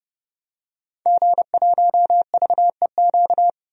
Morse code, heard as a keyed tone, about 30 wpm, 700 Hz. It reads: G1VEQ
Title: DXCC-100 Call Signs - 5 Characters - 30wpm